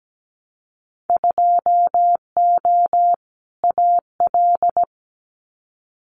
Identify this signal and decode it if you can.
Morse code, heard as a keyed tone, about 17 wpm, 700 Hz. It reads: 2O AL
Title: Winter Field Day Exchanges 17wpm